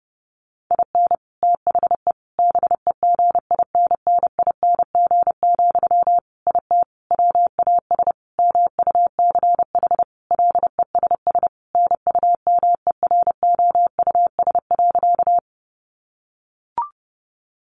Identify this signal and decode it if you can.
Morse code, heard as a keyed tone, about 28 wpm, 700 Hz. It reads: IN THE BEGINNING, IT WAS MUCH LESS NUMEROUS.
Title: Sentences from Top 1400 Words 30wpm